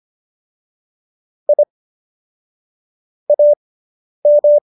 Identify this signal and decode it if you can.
Morse code, heard as a keyed tone, about 25 wpm, spaced as for 10 wpm, 590 Hz. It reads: I AM